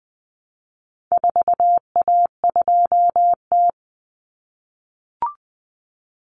Morse code, heard as a keyed tone, about 20 wpm, 700 Hz. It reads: 4A2T
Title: Common Cut Numbers - 4-Digit 20wpm